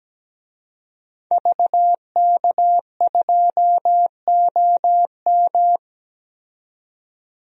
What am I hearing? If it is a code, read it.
Morse code, heard as a keyed tone, about 17 wpm, 700 Hz. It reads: VK2OM
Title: DXCC-100 Call Signs - 5 Characters - 17wpm